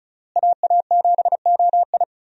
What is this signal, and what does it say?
Morse code, heard as a keyed tone, about 34 wpm, 700 Hz. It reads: AA7OI
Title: USA Call Signs - 5 Characters - Repeated 3x - 35wpm